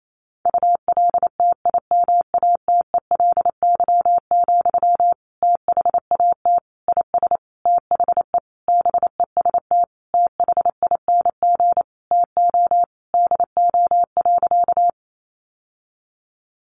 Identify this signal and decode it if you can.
Morse code, heard as a keyed tone, about 28 wpm, 700 Hz. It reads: ULTIMATELY, THAT IS THE BEST THING TO DO.
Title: Sentences from Top 1200 Words 28wpm